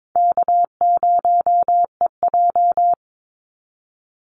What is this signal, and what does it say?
Morse code, heard as a keyed tone, about 22 wpm, 700 Hz. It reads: K0EJ
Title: Easy Call Signs - 22wpm